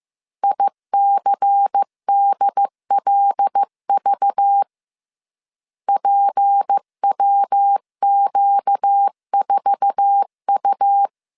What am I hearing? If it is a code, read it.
Morse code, heard as a keyed tone, about 15 wpm, 800 Hz. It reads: ICDLV PWQ4U
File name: CW / DET / 1 9 1500C S A 14wpm 3222